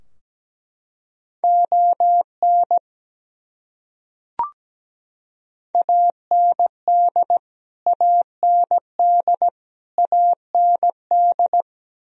Morse code, heard as a keyed tone, about 17 wpm, 700 Hz. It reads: ON AND AND AND
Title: Top 100 Words Repeated 17wpm